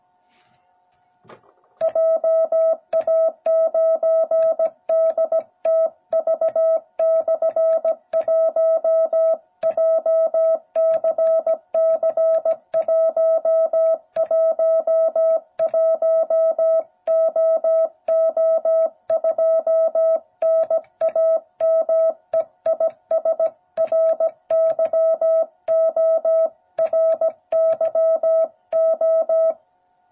Morse code, heard as a keyed tone, about 17 wpm, 660 Hz. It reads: JA9DTV/1 JCC111OO2 NAME IS RYO RYO